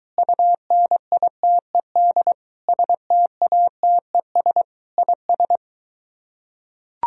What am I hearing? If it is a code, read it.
Morse code, heard as a keyed tone, about 23 wpm, 700 Hz. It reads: UNITED STATES IS